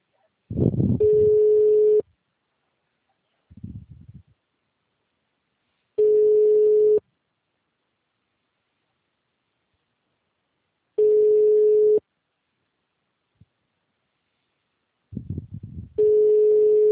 Ringback tone (Europe, 425 Hz).